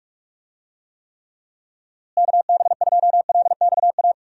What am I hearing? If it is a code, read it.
Morse code, heard as a keyed tone, about 46 wpm, 700 Hz. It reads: KB2LXA
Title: USA Call Signs - 6 Characters - Repeated 3x - 45wpm